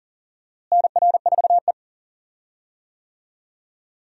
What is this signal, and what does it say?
Morse code, heard as a keyed tone, about 40 wpm, 700 Hz. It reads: NR4E
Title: USA Call Signs - 4 Characters - Repeated 3x - 40wpm